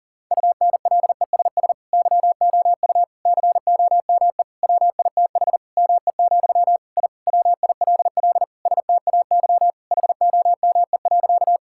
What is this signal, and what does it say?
Morse code, heard as a keyed tone, about 40 wpm, 700 Hz. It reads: UNLESS YOU COME WITH ME, I WILL STAY HOME.
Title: Sentences from Top 1000 Words - Review Part 6 - 40wpm